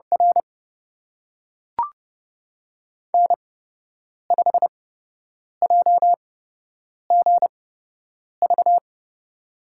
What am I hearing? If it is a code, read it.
Morse code, heard as a keyed tone, about 30 wpm, spaced as for 8 wpm, 700 Hz. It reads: R N5JGV
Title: USA Call Signs - 8x Character Spacing - 30wpm